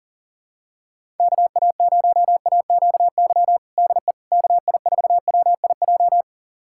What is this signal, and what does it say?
Morse code, heard as a keyed tone, about 40 wpm, 700 Hz. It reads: KA0AQY DE KI4WIJ
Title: USA Callsign DE Callsign - 6 Characters - 40wpm